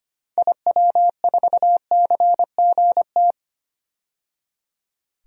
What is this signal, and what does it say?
Morse code, heard as a keyed tone, about 25 wpm, 700 Hz. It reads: IW4CGT
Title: DXCC-100 Call Signs - 6 Characters - 25wpm